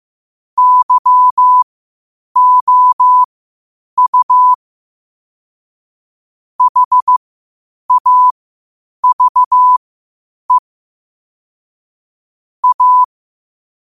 Morse code, heard as a keyed tone, about 15 wpm, spaced as for 8 wpm, 1000 Hz. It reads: YOU HAVE A
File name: qotdfiles / 8wpm